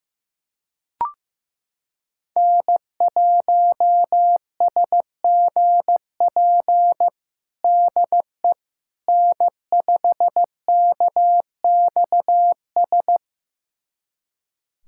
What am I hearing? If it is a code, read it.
Morse code, heard as a keyed tone, about 15 wpm, 700 Hz. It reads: N1SGP DE N5KXS
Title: USA Callsign DE Callsign - 5 Characters - 15wpm